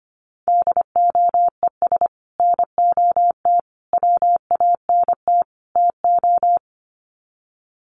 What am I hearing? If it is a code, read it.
Morse code, heard as a keyed tone, about 25 wpm, 700 Hz. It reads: DOES NOT WANT TO
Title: Sets of 4 Words 25wpm